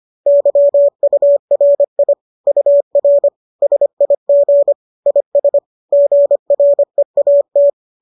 Morse code, heard as a keyed tone, about 25 wpm, 570 Hz. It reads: YURI UR SIG IS GREAT